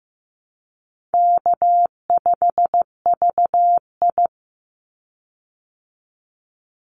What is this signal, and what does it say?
Morse code, heard as a keyed tone, about 15 wpm, 700 Hz. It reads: K5VI
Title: Call Signs Repeated 2x - 15wpm